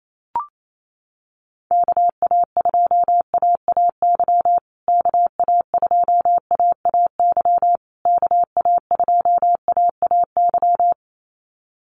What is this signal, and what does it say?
Morse code, heard as a keyed tone, about 28 wpm, 700 Hz. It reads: KA2AAY KA2AAY KA2AAY
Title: USA Call Signs Repeated 3x 28wpm